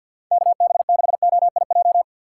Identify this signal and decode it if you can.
Morse code, heard as a keyed tone, about 46 wpm, 700 Hz. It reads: KB6OIJ
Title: USA Call Signs - 6 Characters - 3x Character Spacing - 50wpm